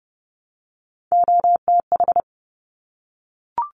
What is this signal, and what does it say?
Morse code, heard as a keyed tone, about 30 wpm, 700 Hz. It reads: OTH
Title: Top Three Letter Combinations in Words 30wpm